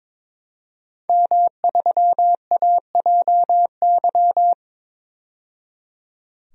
Morse code, heard as a keyed tone, about 22 wpm, 700 Hz. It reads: M3AJY
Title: DXCC-100 Call Signs - 5 Characters - 22wpm